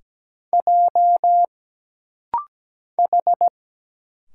Morse code, heard as a keyed tone, about 17 wpm, 700 Hz. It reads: J H
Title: Single Letters - Rapid-Fire 17wpm